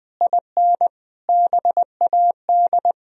Morse code, heard as a keyed tone, about 20 wpm, 700 Hz. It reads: IN BAD
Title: Sentences from Top 1000 Words - Review Part 1 - 20wpm